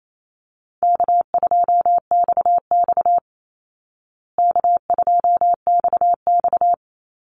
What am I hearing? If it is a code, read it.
Morse code, heard as a keyed tone, about 28 wpm, 700 Hz. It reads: K2XX K2XX
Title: Easy Call Signs Repeated 3x - 28wpm